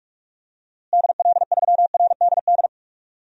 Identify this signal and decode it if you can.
Morse code, heard as a keyed tone, about 46 wpm, 700 Hz. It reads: DL3RDD
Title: DXCC-100 Call Signs - 6 Characters - Repeated 3x - 45wpm